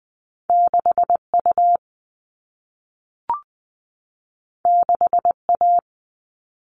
Morse code, heard as a keyed tone, about 20 wpm, 700 Hz. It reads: 6U 6A
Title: All Cut Numbers - 2-Digit 20wpm